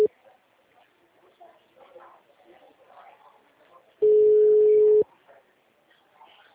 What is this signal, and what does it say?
Ringback tone (Europe, 425 Hz).